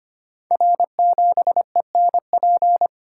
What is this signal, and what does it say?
Morse code, heard as a keyed tone, about 25 wpm, 700 Hz. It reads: R7ENP
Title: Code Groups - 5 Letter:Numbers - 25wpm